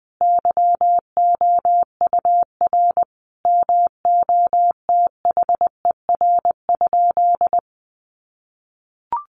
Morse code, heard as a keyed tone, about 20 wpm, 700 Hz. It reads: YOUR MOTHER?
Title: Sentences from Top 1100 Words 20wpm